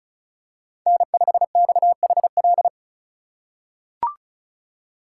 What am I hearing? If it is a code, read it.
Morse code, heard as a keyed tone, about 34 wpm, 700 Hz. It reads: N5XHL
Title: USA Call Signs - 5 Characters - 3x Character Spacing - 35wpm